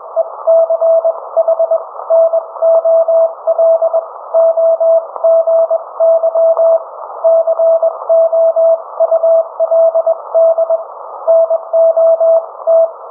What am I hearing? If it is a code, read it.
Morse code, heard as a keyed tone, about 21 wpm, spaced as for 18 wpm, 660 Hz. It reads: ECHNOLOGY COULD NOT